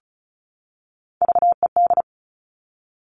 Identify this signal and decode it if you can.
Morse code, heard as a keyed tone, about 36 wpm, 700 Hz. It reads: VED
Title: Top Three Letter Combinations in Words 35wpm